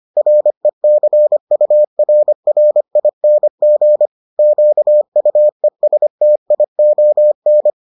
Morse code, heard as a keyed tone, about 25 wpm, 600 Hz. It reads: RECURRING QUESTION